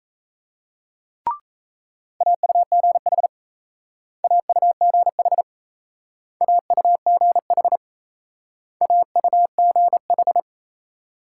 Morse code, heard as a keyed tone, about 38 wpm, 700 Hz. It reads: AUGH AUGH AUGH AUGH